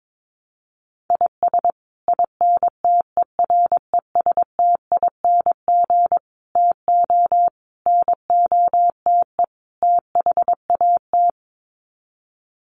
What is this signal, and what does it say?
Morse code, heard as a keyed tone, about 22 wpm, 700 Hz. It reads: IS INTERESTING TO NOTE THAT
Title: Sets of 5 Words Encore2 22wpm